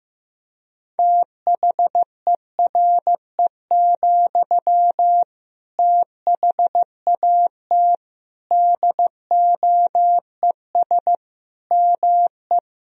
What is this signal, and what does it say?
Morse code, heard as a keyed tone, about 15 wpm, 700 Hz. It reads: THERE, THAT DOES ME
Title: Sentences from Top 200 Words and Abbreviations 15wpm